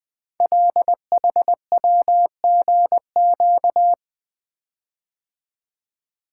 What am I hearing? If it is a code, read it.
Morse code, heard as a keyed tone, about 20 wpm, 700 Hz. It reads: LHWGQ
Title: Code Groups - 5 Letters - 20wpm